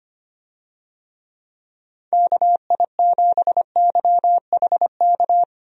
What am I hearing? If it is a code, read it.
Morse code, heard as a keyed tone, about 25 wpm, 700 Hz. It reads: KI7YHK
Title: USA Call Signs - 6 Characters - Repeated 3x - 25wpm